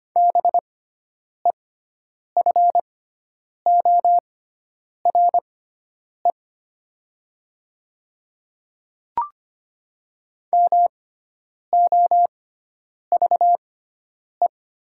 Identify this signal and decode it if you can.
Morse code, heard as a keyed tone, about 25 wpm, spaced as for 9 wpm, 700 Hz. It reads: BEFORE MOVE
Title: 6x Character Spacing - Top 200 Words - 25wpm